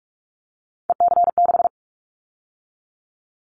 Morse code, heard as a keyed tone, about 44 wpm, 700 Hz. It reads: EC6
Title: Callsign Prefixes for Top 50 DXCC - 45wpm